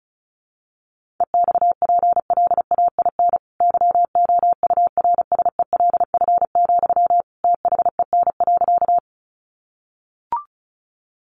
Morse code, heard as a keyed tone, about 34 wpm, 700 Hz. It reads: EXPLAIN YOURSELF, THEN.
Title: Sentences from Top 1100 Words 35wpm